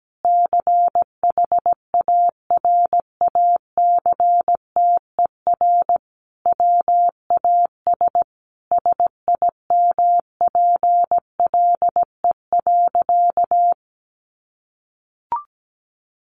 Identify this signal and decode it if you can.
Morse code, heard as a keyed tone, about 17 wpm, 700 Hz. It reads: CHARACTER WAS SIMPLE.
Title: Sentences from Top 1200 Words 17wpm